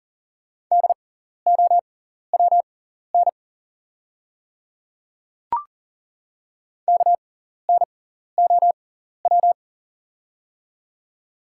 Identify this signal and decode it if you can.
Morse code, heard as a keyed tone, about 39 wpm, spaced as for 14 wpm, 700 Hz. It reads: DOWN KNOW
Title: 6x Character Spacing - Top 100 Words - 40wpm